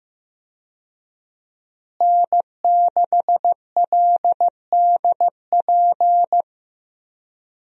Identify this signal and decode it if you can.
Morse code, heard as a keyed tone, about 15 wpm, 700 Hz. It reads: N6LDP
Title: USA Call Signs - 5 Characters - Repeated 2x - 15wpm